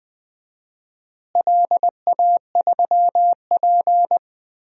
Morse code, heard as a keyed tone, about 21 wpm, 700 Hz. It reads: LA3P
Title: DXCC-100 Call Signs - 4 Characters - Repeated 3x - 20wpm